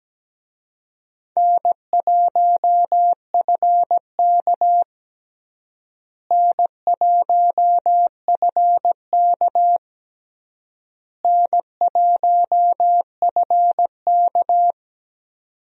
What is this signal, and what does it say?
Morse code, heard as a keyed tone, about 17 wpm, 700 Hz. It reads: N1FK N1FK N1FK
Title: USA Call Signs - 4 Characters - Repeated 3x - 17wpm